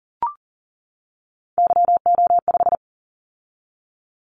Morse code, heard as a keyed tone, about 40 wpm, 700 Hz. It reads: YO5
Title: Callsign Prefixes for Top 25 DXCC - 40wpm